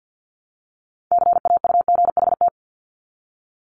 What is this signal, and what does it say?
Morse code, heard as a keyed tone, about 50 wpm, 700 Hz. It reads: CAUGHT
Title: Multi-letter Phonemes Repeated 3x 50wpm